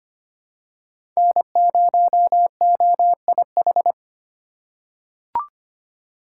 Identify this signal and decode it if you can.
Morse code, heard as a keyed tone, about 25 wpm, 700 Hz. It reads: N0OIH
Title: USA Call Signs - 5 Characters - 2x Character Spacing - 25wpm